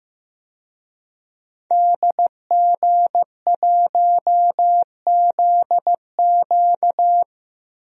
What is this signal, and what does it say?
Morse code, heard as a keyed tone, about 15 wpm, 700 Hz. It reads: DG1ZQ
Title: DXCC-100 Call Signs - 5 Characters - Repeated 2x - 15wpm